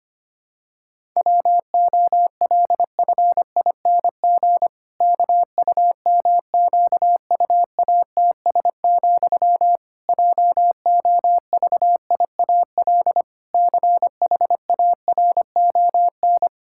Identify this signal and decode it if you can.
Morse code, heard as a keyed tone, about 25 wpm, 700 Hz. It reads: WOLFING KUMQUATS, JOVIAL CHARON
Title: Pangram Challenge v2 - 25wpm